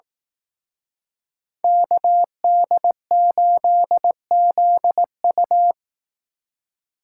Morse code, heard as a keyed tone, about 19 wpm, 700 Hz. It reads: KD8ZU